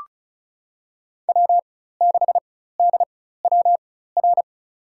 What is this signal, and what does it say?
Morse code, heard as a keyed tone, about 36 wpm, spaced as for 16 wpm, 700 Hz. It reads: W6DWR
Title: USA Call Signs - 5 Characters - 4x Character Spacing - 35wpm